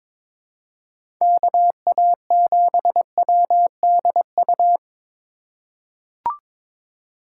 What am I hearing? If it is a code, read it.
Morse code, heard as a keyed tone, about 22 wpm, 700 Hz. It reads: KA7WDU
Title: USA Call Signs - 6 Characters - 3x Character Spacing - 22wpm